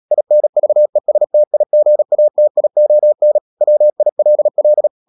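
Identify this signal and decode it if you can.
Morse code, heard as a keyed tone, about 37 wpm, 600 Hz. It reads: INVESTIGATION WILL
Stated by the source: Morse code practice at 35–39 wpm sending English text of Ken Follett's novel Never, from CW med Gnister.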